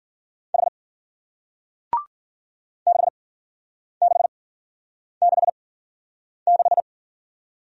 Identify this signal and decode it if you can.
Morse code, heard as a keyed tone, about 52 wpm, spaced as for 9 wpm, 700 Hz. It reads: H 6666